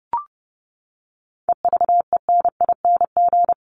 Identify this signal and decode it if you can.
Morse code, heard as a keyed tone, about 30 wpm, 700 Hz. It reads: EVENING
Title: Top 1501-1750 Words 30wpm